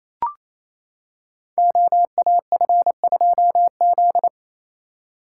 Morse code, heard as a keyed tone, about 28 wpm, 700 Hz. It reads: OAF2Z
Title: Code Groups - 5 Letter:Numbers - 28wpm